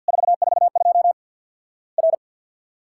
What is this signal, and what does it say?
Morse code, heard as a keyed tone, about 49 wpm, 700 Hz. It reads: 442 R
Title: Worked All Europe DX Contest - QTC Exchange with Repeats - 50wpm